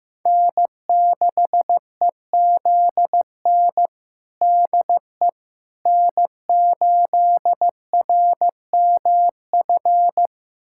Morse code, heard as a keyed tone, about 16 wpm, 700 Hz. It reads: N6EZN DE N8RMF
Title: USA Callsign DE Callsign - 5 Characters - 15wpm